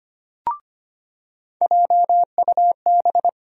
Morse code, heard as a keyed tone, about 25 wpm, 700 Hz. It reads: JUB
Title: USA Call Sign Postfixes - 3 Letters 25wpm